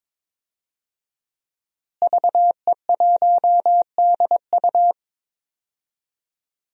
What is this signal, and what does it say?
Morse code, heard as a keyed tone, about 21 wpm, 700 Hz. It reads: VE1DU
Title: DXCC-100 Call Signs - 5 Characters - Repeated 2x - 22wpm